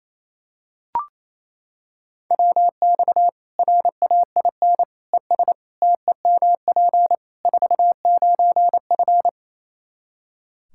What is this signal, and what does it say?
Morse code, heard as a keyed tone, about 28 wpm, 700 Hz. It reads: WX RAIN ES TEMP 49F
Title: Practice QSO - Exchange 2 - Info - 28wpm